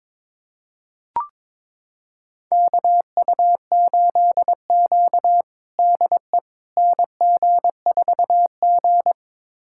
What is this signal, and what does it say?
Morse code, heard as a keyed tone, about 22 wpm, 700 Hz. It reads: KU8Q DE NG4G
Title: USA Callsign DE Callsign - 4 Characters - 22wpm